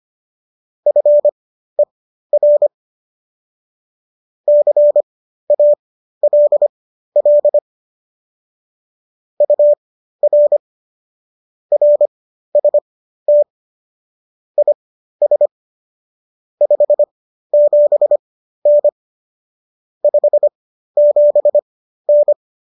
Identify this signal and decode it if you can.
Morse code, heard as a keyed tone, about 25 wpm, spaced as for 13 wpm, 590 Hz. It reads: FER CALL UR RST IS 57N 57N
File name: qso219 13